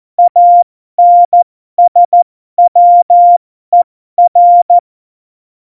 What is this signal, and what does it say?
Morse code, heard as a keyed tone, about 14 wpm, spaced as for 12 wpm, 700 Hz. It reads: ANSWER
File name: Morsekurs / Witze